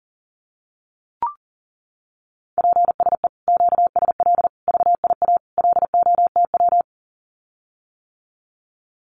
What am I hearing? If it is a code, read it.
Morse code, heard as a keyed tone, about 38 wpm, 700 Hz. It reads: PSE QSL VIA LOTW
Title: Common QSO Phrases Encore 40wpm